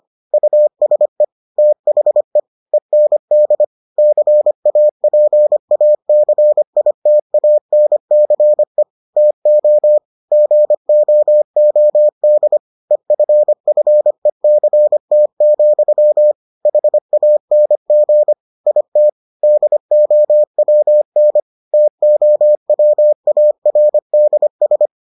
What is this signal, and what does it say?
Morse code, heard as a keyed tone, about 25 wpm, 600 Hz. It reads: USE THE END CAPACITANCE TO GOOD EFFECT, HANG IT DOWN TOWARDS